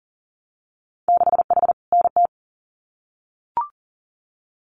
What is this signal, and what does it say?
Morse code, heard as a keyed tone, about 40 wpm, 700 Hz. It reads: 6H NT
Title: Winter Field Day Exchanges Encore 40wpm